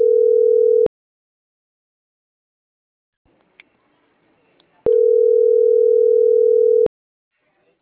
Ringback tone (North America), 440 + 480 Hz.